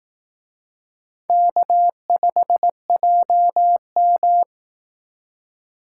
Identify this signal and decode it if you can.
Morse code, heard as a keyed tone, about 18 wpm, 700 Hz. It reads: K5JM
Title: Speed Racing - USA 1x2 Call Signs 15wpm